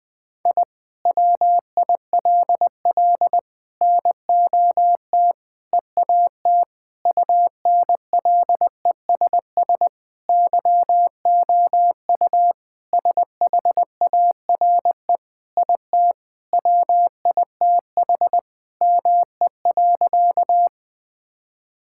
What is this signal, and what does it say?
Morse code, heard as a keyed tone, about 20 wpm, 700 Hz. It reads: I WILL NOT EAT UNLESS YOU SHARE IT WITH ME.
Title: Sentences from Top 1000 Words - Review Part 6 - 20wpm